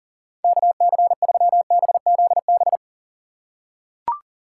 Morse code, heard as a keyed tone, about 40 wpm, 700 Hz. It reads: KC3BZB
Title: USA Call Signs - 6 Characters - 3x Character Spacing - 40wpm